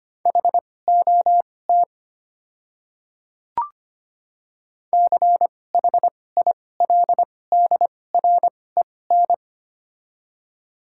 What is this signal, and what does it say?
Morse code, heard as a keyed tone, about 25 wpm, spaced as for 18 wpm, 700 Hz. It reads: HOT CHILDREN
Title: 2x Character Spacing - Top 500 Words - 25wpm